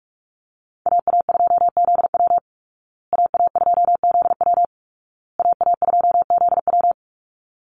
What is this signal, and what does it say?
Morse code, heard as a keyed tone, about 44 wpm, 700 Hz. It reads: AA2ZW AA2ZW AA2ZW
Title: Call Signs Repeated 2x - 45wpm